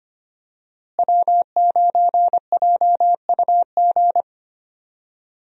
Morse code, heard as a keyed tone, about 25 wpm, 700 Hz. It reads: W9JUG
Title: USA Call Signs - 5 Characters - Repeated 3x - 25wpm